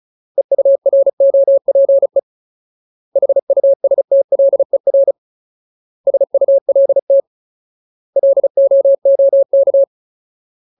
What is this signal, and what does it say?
Morse code, heard as a keyed tone, about 34 wpm, 550 Hz. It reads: EUROPE HUSTLER SULT LOOK
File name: Ultimate Word List / 35WPM Ultimate